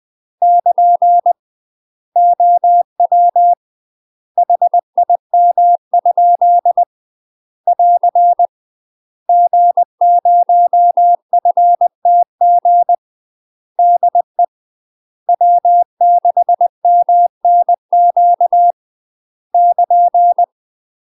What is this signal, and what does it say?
Morse code, heard as a keyed tone, about 20 wpm, 700 Hz. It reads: <KN> OW HIM? <AR> G0FTG DE W6MNQ <KN>
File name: UK MASTER QSOs TRAINING / 20WPM / UK MASTER QSOs 0058